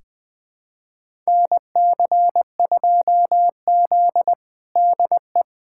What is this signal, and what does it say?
Morse code, heard as a keyed tone, about 21 wpm, 700 Hz. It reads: NC2Z DE
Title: USA Callsign DE Callsign - 4 Characters - 20wpm